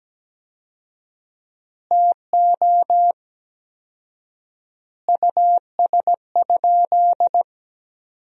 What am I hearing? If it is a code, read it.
Morse code, heard as a keyed tone, about 17 wpm, 700 Hz. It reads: TO US?
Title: 4x Word-Spacing - Sentences from Top 400 Words - 17wpm